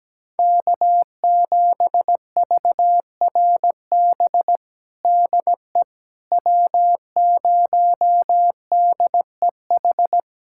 Morse code, heard as a keyed tone, about 17 wpm, 700 Hz. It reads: K7VRB DE W0DEH
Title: USA Callsign DE Callsign - 5 Characters - 17wpm